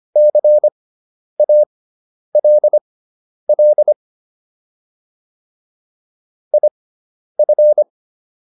Morse code, heard as a keyed tone, about 25 wpm, spaced as for 10 wpm, 600 Hz. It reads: CALL IF